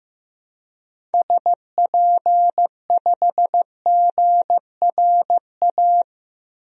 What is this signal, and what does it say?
Morse code, heard as a keyed tone, about 15 wpm, 700 Hz. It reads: SP5GRA
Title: DXCC-100 Call Signs - 6 Characters - Repeated 2x - 15wpm